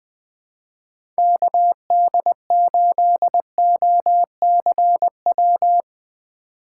Morse code, heard as a keyed tone, about 20 wpm, 700 Hz. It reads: KD8OCW
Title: 2023 POTA Hunter Call Signs - 20wpm